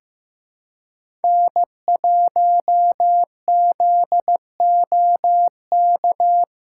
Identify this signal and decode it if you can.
Morse code, heard as a keyed tone, about 16 wpm, 700 Hz. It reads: N1ZOK
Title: USA Callsign DE Callsign - 5 Characters - 15wpm